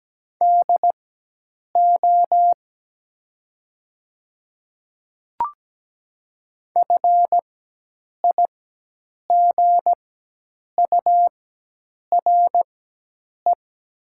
Morse code, heard as a keyed tone, about 17 wpm, spaced as for 8 wpm, 700 Hz. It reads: DO FIGURE
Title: 4x Character Spacing - Top 500 Words - 17wpm